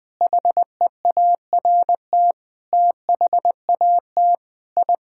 Morse code, heard as a keyed tone, about 21 wpm, 700 Hz. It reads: HEART THAT I
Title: Sets of 8 Words - 20wpm